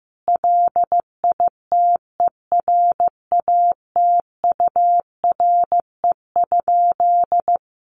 Morse code, heard as a keyed tone, about 15 wpm, 700 Hz. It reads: LITERATURE?
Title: Sentences from Top 1000 Words 15wpm